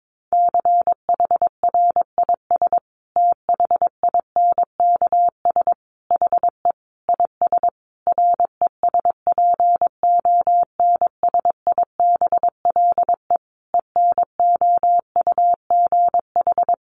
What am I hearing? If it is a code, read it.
Morse code, heard as a keyed tone, about 22 wpm, 700 Hz. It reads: CHRIS THINKS HE IS RESPONSIBLE ENOUGH
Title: Easy Phrases 22wpm